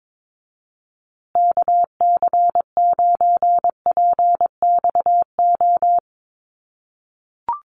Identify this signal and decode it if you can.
Morse code, heard as a keyed tone, about 22 wpm, 700 Hz. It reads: KC9PXO
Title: USA Call Signs Repeated 3x 22wpm